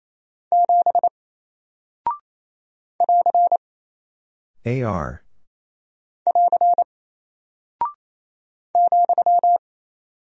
Morse code, heard as a keyed tone, about 28 wpm, 700 Hz. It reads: 7 <AR> <AR> ,
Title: Single Character (Letter:Number:Punctuation:Prosign) Alt 28wpm